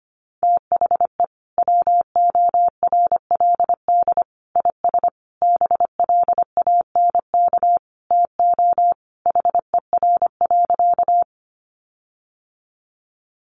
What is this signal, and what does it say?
Morse code, heard as a keyed tone, about 25 wpm, 700 Hz. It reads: THE WORLD IS BLANK TO HER.
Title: Sentences from Top 1500 Words 25wpm